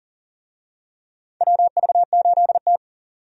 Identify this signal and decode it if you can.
Morse code, heard as a keyed tone, about 40 wpm, 700 Hz. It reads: WV8T
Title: USA Call Signs - 4 Characters - 40wpm